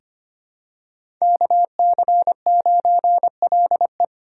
Morse code, heard as a keyed tone, about 25 wpm, 700 Hz. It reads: KC9LE
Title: USA Call Signs - 5 Characters - 2x Character Spacing - 25wpm